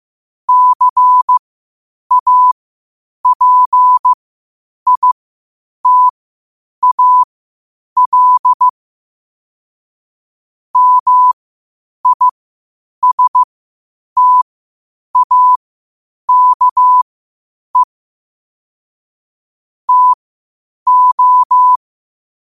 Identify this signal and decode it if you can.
Morse code, heard as a keyed tone, about 15 wpm, spaced as for 8 wpm, 1000 Hz. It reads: CAPITAL MISTAKE TO
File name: qotdfiles / 8wpm